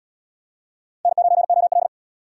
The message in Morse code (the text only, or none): N9OL